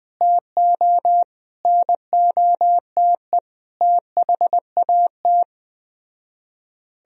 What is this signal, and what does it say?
Morse code, heard as a keyed tone, about 20 wpm, 700 Hz. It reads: TO NOTE THAT
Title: Speed-Racing - Sets of 5 Words 17wpm